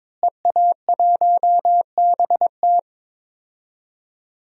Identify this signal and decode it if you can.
Morse code, heard as a keyed tone, about 22 wpm, 700 Hz. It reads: EA1BT
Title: DXCC-100 Call Signs - 5 Characters - Repeated 2x - 22wpm